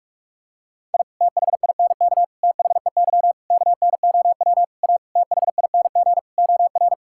Morse code, heard as a keyed tone, about 46 wpm, 700 Hz. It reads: I THINK THEY KNOW A THING OR
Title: Sentences from Top 1000 Words - Review Part 5 - 45wpm